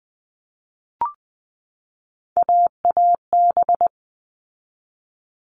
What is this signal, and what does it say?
Morse code, heard as a keyed tone, about 20 wpm, 700 Hz. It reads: AAB
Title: All Cut Numbers - 3-Digit 20wpm